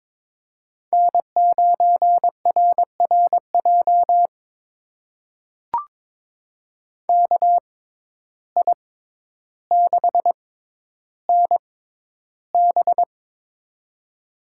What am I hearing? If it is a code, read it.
Morse code, heard as a keyed tone, about 21 wpm, spaced as for 11 wpm, 700 Hz. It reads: N9RRJ KI6NB
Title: USA Call Signs - 5 Characters - 6x Character Spacing - 22wpm